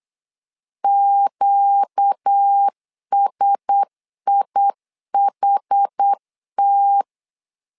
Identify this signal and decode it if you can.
Morse code, heard as a keyed tone, about 8 wpm, 800 Hz. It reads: QSIHT
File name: LEZIONI / 4 Q1S 8wpm 2710